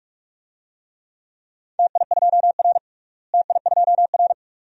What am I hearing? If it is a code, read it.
Morse code, heard as a keyed tone, about 43 wpm, 700 Hz. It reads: TI2R TI2R
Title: DXCC-100 Call Signs - 4 Characters - Repeated 2x - 45wpm